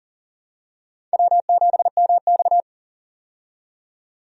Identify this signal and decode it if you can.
Morse code, heard as a keyed tone, about 40 wpm, 700 Hz. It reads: W7MX
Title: USA Call Signs - 4 Characters - 40wpm